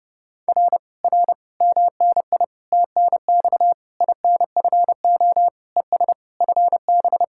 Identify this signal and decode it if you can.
Morse code, heard as a keyed tone, about 30 wpm, 700 Hz. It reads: R R MNI TNX INFO ES FB